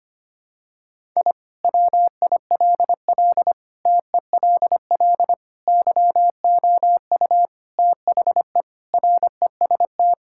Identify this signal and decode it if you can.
Morse code, heard as a keyed tone, about 25 wpm, 700 Hz. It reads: I WILL TELL YOU THE REST
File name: practice / 25wpm / Sentences with 4-Word Phrases - 25wpm